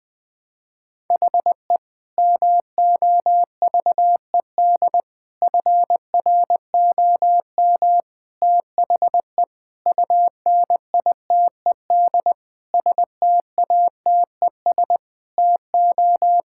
Morse code, heard as a keyed tone, about 21 wpm, 700 Hz. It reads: HE MOVED FROM THE UNITED STATES TO
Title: Sentences with 4-Word Phrases - Spoken Beginning - 20wpm